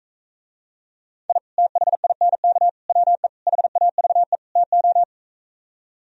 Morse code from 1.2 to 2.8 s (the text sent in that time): I THINK